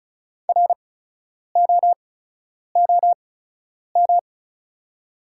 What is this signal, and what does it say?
Morse code, heard as a keyed tone, about 36 wpm, spaced as for 10 wpm, 690 Hz. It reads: ROOM
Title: 8x Character Spacing - Top 500 Words - 35wpm